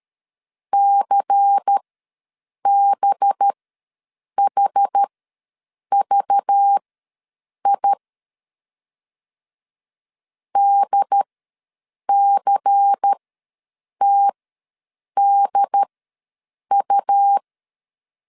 Morse code, heard as a keyed tone, about 13 wpm, spaced as for 7 wpm, 800 Hz. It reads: CBHVI DCTDU
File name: LEZIONI / 5 LBVC / 5 LBVC ws 12wpm 3159